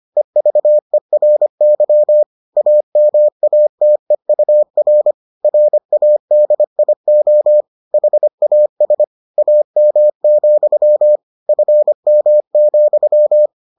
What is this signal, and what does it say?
Morse code, heard as a keyed tone, about 25 wpm, 600 Hz. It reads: EVERY AMATEUR RADIO HAS AM, FM,